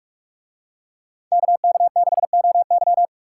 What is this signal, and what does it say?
Morse code, heard as a keyed tone, about 45 wpm, 700 Hz. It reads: KK6OY